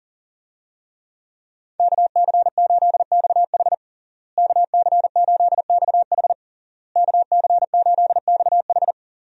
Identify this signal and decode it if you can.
Morse code, heard as a keyed tone, about 40 wpm, 700 Hz. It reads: KC8XH KC8XH KC8XH
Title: USA Call Signs - 5 Characters - Repeated 3x - 40wpm